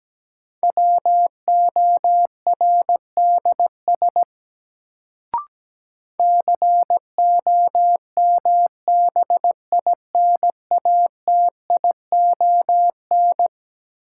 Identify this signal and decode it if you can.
Morse code, heard as a keyed tone, about 17 wpm, 700 Hz. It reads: WORDS COMBINATION
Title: Sentences from Top 1000 Words - Review Part 6 - New Word Drill - 17wpm